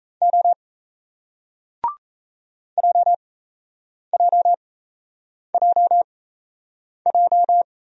Morse code, heard as a keyed tone, about 36 wpm, spaced as for 8 wpm, 700 Hz. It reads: O JJJJ